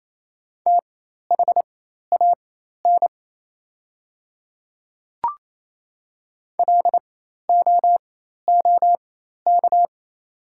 Morse code, heard as a keyed tone, about 27 wpm, spaced as for 13 wpm, 690 Hz. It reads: THAN LOOK
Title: 4x Character Spacing - Top 100 Words - 28wpm